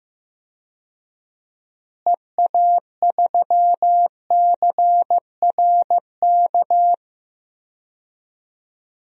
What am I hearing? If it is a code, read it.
Morse code, heard as a keyed tone, about 15 wpm, 700 Hz. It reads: EA3CRK
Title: DXCC-100 Call Signs - 6 Characters - Repeated 2x - 15wpm